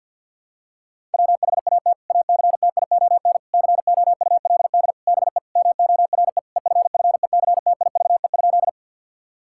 Morse code, heard as a keyed tone, about 50 wpm, 700 Hz. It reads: WHAT ACTION COULD BE MORE EFFECTIVE?
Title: Sentences from Top 1000 Words - Review Part 6 - 50wpm